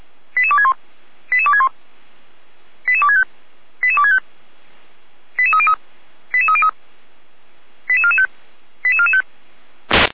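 Five-tone selective call (ZVEI), heard as quick runs of five tones, tones of about 80 ms. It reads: 88261 88261 88265 88265 88303 88303 88445 88445
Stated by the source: ZVEI five-tone selective call received on SDR.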